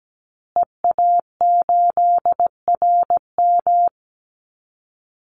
Morse code, heard as a keyed tone, about 17 wpm, 700 Hz. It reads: EA8RM
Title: Call Signs Repeated 2x - 17wpm